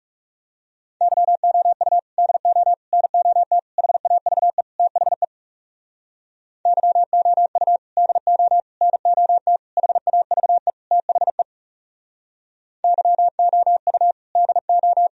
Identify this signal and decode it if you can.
Morse code, heard as a keyed tone, about 38 wpm, 700 Hz. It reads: YOU DO NOT HAVE THE YOU DO NOT HAVE THE YOU DO